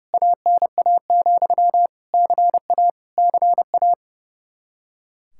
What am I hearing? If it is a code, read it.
Morse code, heard as a keyed tone, about 30 wpm, 700 Hz. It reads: ANA, CA CA
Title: Practice QSO - Exchange 1 - Beginning - 30wpm